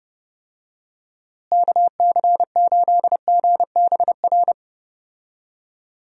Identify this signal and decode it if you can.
Morse code, heard as a keyed tone, about 30 wpm, 700 Hz. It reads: KC8GBR
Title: USA Call Signs - 6 Characters - Repeated 3x - 30wpm